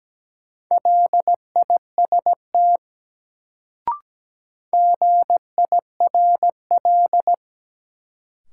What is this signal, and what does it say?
Morse code, heard as a keyed tone, about 17 wpm, 700 Hz. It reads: LIST GIRL
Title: Sentences from Top 400 Words - New Word Drill - 17wpm